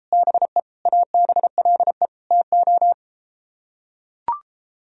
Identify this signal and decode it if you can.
Morse code, heard as a keyed tone, about 33 wpm, 700 Hz. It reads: BE ABLE TO